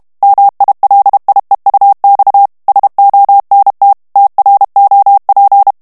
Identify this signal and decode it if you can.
Morse code, heard as a keyed tone, about 32 wpm, 800 Hz. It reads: MILIEUX SONT TROP
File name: CW / FICHIERS-MP3 / FRANCAIS TEXTE CLAIR 2 VITESSE 30 MOTS MINUTES